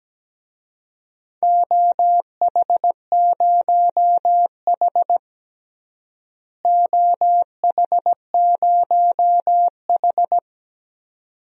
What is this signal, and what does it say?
Morse code, heard as a keyed tone, about 17 wpm, 700 Hz. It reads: OH0H OH0H
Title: DXCC-100 Call Signs - 4 Characters - Repeated 2x - 17wpm